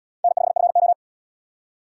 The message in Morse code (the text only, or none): N5FO